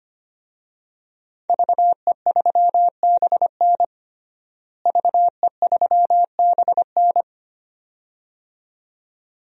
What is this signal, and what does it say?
Morse code, heard as a keyed tone, about 25 wpm, 700 Hz. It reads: VE3BN VE3BN
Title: DXCC-100 Call Signs - 5 Characters - Repeated 2x - 25wpm